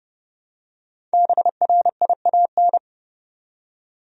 Morse code, heard as a keyed tone, about 30 wpm, 700 Hz. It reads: BRIAN
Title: USA Common Names - Max 5 Letters - 30wpm